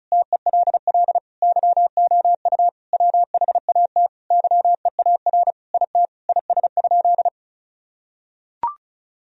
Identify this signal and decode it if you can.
Morse code, heard as a keyed tone, about 34 wpm, 700 Hz. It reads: TELL YOU WHAT YEAR IT IS?
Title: Sentences from Top 1000 Words - Review Part 1 - 35wpm